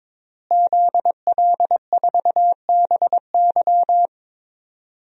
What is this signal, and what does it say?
Morse code, heard as a keyed tone, about 22 wpm, 700 Hz. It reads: ZL4BY
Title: DXCC-100 Call Signs - 5 Characters - Repeated 2x - 22wpm